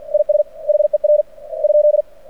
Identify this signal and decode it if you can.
Morse code, heard as a keyed tone, about 49 wpm, 600 Hz. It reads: IS HER ?
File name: archived / CRSnet